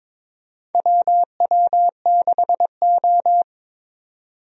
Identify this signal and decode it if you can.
Morse code, heard as a keyed tone, about 22 wpm, 700 Hz. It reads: WW6O
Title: USA Call Signs - 4 Characters - 22wpm